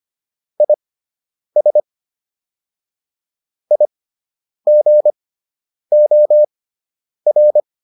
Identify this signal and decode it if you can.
Morse code, heard as a keyed tone, about 25 wpm, spaced as for 9 wpm, 600 Hz. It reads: IS IGOR